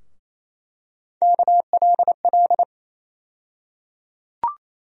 Morse code, heard as a keyed tone, about 28 wpm, 700 Hz. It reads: KLL
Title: USA Call Sign Postfixes - 3 Letters 28wpm